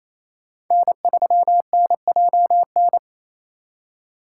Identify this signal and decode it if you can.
Morse code, heard as a keyed tone, about 32 wpm, 700 Hz. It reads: N3NJN
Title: Speed Racing - USA 1x3 Call Signs 28wpm